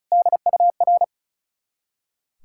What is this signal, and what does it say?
Morse code, heard as a keyed tone, about 36 wpm, 690 Hz. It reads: DUR
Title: Ontario QSO Party Exchange 35wpm